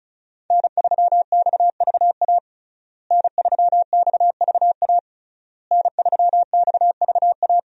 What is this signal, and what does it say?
Morse code, heard as a keyed tone, about 36 wpm, 700 Hz. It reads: N3XVA N3XVA N3XVA
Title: USA Call Signs - 5 Characters - Repeated 3x - 35wpm